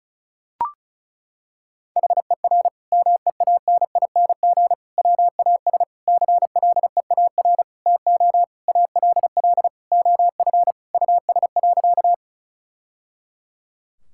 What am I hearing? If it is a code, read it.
Morse code, heard as a keyed tone, about 36 wpm, 700 Hz. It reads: HER MEANING WAS CLEAR TO ALL OF US.
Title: Sentences from Top 1000 Words - Review Part 6 - 35wpm